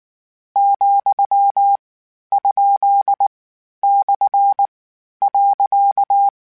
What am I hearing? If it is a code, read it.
Morse code, heard as a keyed tone, about 19 wpm, 800 Hz. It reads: , ? / .
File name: Misc / Misc-mp3s / Punct @18